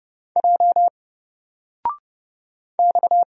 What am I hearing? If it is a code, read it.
Morse code, heard as a keyed tone, about 30 wpm, 700 Hz. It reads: J X
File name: Single-Letter-Number-30wpm